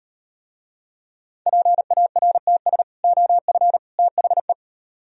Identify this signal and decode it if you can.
Morse code, heard as a keyed tone, about 38 wpm, 700 Hz. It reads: PARTS OF THE